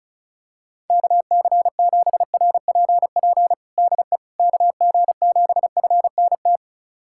Morse code, heard as a keyed tone, about 36 wpm, 700 Hz. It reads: KC7RPP DE KG7FNT
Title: USA Callsign DE Callsign - 6 Characters - 35wpm